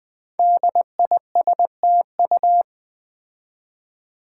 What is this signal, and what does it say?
Morse code, heard as a keyed tone, about 20 wpm, 700 Hz. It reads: DISTU
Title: Build A Word - 5 Letters At A Time - 10-Letter Words - 20wpm